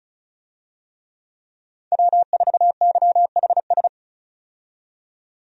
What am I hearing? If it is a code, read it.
Morse code, heard as a keyed tone, about 36 wpm, 700 Hz. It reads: W4YHS
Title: USA Call Signs - 5 Characters - 35wpm